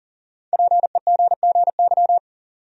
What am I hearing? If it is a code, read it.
Morse code, heard as a keyed tone, about 37 wpm, 700 Hz. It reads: PEGGY
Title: USA Common Names - Max 5 Letters - 40wpm